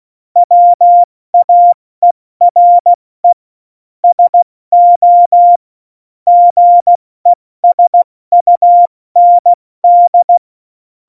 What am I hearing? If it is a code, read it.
Morse code, heard as a keyed tone, about 16 wpm, 700 Hz. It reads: WAERE SO GESUND
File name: Morsekurs / Witze / Witz 28 Juli 2020, 70 F80